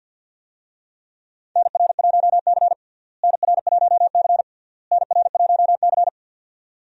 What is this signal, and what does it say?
Morse code, heard as a keyed tone, about 52 wpm, 700 Hz. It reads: NR1C NR1C NR1C
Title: USA Call Signs - 4 Characters - Repeated 3x - 50wpm